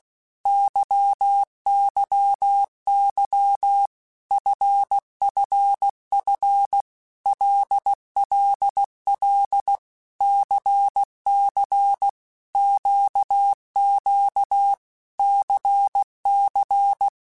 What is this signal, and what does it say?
Morse code, heard as a keyed tone, about 16 wpm, 800 Hz. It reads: YYY FFF LLL CC QQ CC
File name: Basic Letter Groups By7Groups / LFCYQ / LFCYQ@15